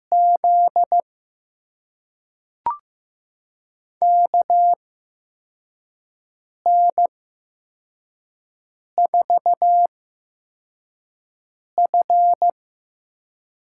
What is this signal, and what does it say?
Morse code, heard as a keyed tone, about 15 wpm, spaced as for 4 wpm, 700 Hz. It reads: Z KN4F
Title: USA Call Signs - 6 Characters - 8x Character Spacing - 15wpm